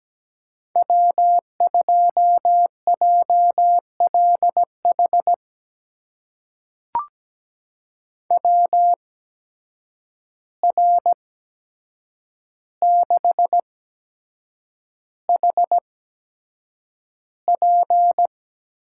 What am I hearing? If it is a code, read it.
Morse code, heard as a keyed tone, about 17 wpm, spaced as for 7 wpm, 700 Hz. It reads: W2JLH WR6HP
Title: USA Call Signs - 5 Characters - 8x Character Spacing - 17wpm